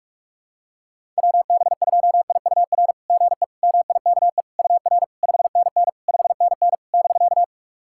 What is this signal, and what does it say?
Morse code, heard as a keyed tone, about 41 wpm, 690 Hz. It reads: WB2IUR GE MIKE UR 5NN 5NN <BK>
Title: SOTA QSO Practice 45wpm